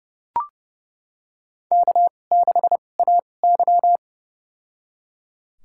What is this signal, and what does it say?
Morse code, heard as a keyed tone, about 31 wpm, spaced as for 22 wpm, 700 Hz. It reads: K6AY
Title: USA Call Signs - 4 Characters - 2x Character Spacing - 30wpm